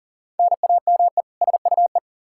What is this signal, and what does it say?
Morse code, heard as a keyed tone, about 37 wpm, 700 Hz. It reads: NAME SUE
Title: Practice QSO - Exchange 1 - Beginning - 40wpm